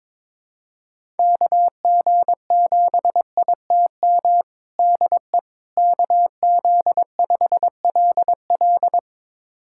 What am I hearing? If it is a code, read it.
Morse code, heard as a keyed tone, about 22 wpm, 700 Hz. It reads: KG7ITM DE KZ5LL
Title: USA Callsign DE Callsign 22wpm